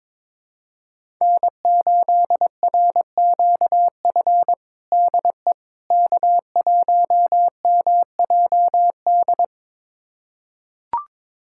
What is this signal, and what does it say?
Morse code, heard as a keyed tone, about 21 wpm, 700 Hz. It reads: N8RQF DE K1MJD
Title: USA Callsign DE Callsign - 5 Characters - 22wpm